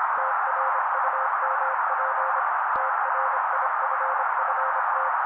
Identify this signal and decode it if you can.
Morse code, heard as a keyed tone, about 26 wpm, 530 Hz. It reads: TRUMP TRIFFT
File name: daily / 25wpm noise